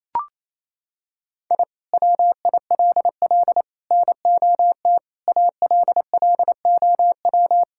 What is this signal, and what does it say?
Morse code, heard as a keyed tone, about 28 wpm, 700 Hz. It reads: I WILL NOT ALLOW
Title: Sentences from Top 1000 Words - Review Part 5 - 28wpm